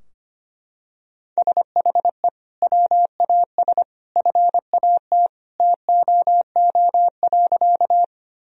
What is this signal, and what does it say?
Morse code, heard as a keyed tone, about 25 wpm, 700 Hz. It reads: SHE WAS FAT TOO.
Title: Sentences from Top 1000 Words - Review Part 4 - 25wpm